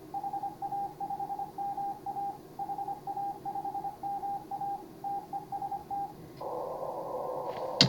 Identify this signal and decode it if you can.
Morse code, heard as a keyed tone, about 25 wpm, 810 Hz. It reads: HA5MA HA5MA TEST
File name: CW / HA5MA RHOB HANG CW 2017